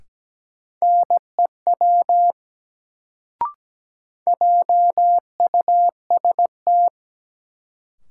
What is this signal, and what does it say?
Morse code, heard as a keyed tone, about 17 wpm, 700 Hz. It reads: NEW JUST
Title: Sets of 2 Words - New Word Drill - 17wpm